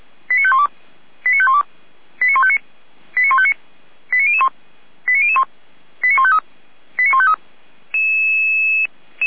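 Five-tone selective call (ZVEI), heard as quick runs of five tones, tones of about 70 ms. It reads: 89512 89512 89159 89159 89001 89001 89153 89153
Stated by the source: ZVEI five-tone call 89150 received on SDR.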